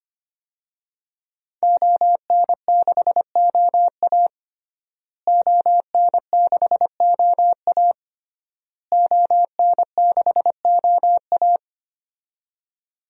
Morse code, heard as a keyed tone, about 25 wpm, 700 Hz. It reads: ON6OA ON6OA ON6OA
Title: DXCC-100 Call Signs - 5 Characters - Repeated 3x - 25wpm